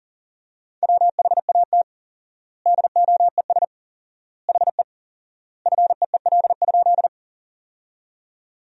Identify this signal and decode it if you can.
Morse code, heard as a keyed tone, about 40 wpm, 700 Hz. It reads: WHAT DOES HE FEEL?
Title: 4x Word-Spacing - Sentences from Top 400 Words - 40wpm